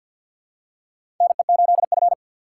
Morse code, heard as a keyed tone, about 51 wpm, 690 Hz. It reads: NE8F
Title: USA Call Signs - 4 Characters - 6x Character Spacing - 50wpm